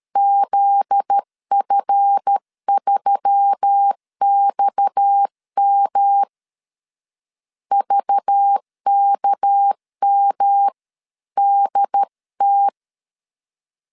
Morse code, heard as a keyed tone, about 13 wpm, 800 Hz. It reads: ZF3XM VKMDT
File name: LEZIONI / 7 GF9Z / 7 GF9Z 12wpm 2147